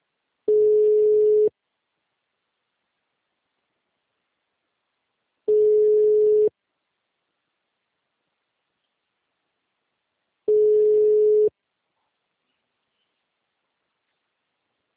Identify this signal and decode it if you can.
Ringback tone (Europe, 425 Hz).